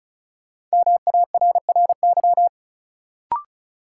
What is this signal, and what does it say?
Morse code, heard as a keyed tone, about 34 wpm, 700 Hz. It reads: MARRY
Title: Sentences from Top 1200 Words - New Word Drill - 35wpm